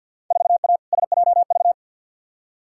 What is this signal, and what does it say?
Morse code, heard as a keyed tone, about 53 wpm, 700 Hz. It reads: 4A SJV
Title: Field Day Exchanges Encore - 50wpm